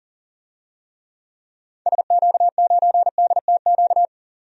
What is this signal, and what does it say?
Morse code, heard as a keyed tone, about 40 wpm, 700 Hz. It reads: SQ9DTQ
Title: DXCC-100 Call Signs - 6 Characters - 40wpm